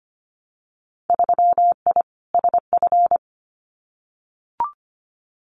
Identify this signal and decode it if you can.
Morse code, heard as a keyed tone, about 25 wpm, 700 Hz. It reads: 3I SF
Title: Winter Field Day Exchanges 25wpm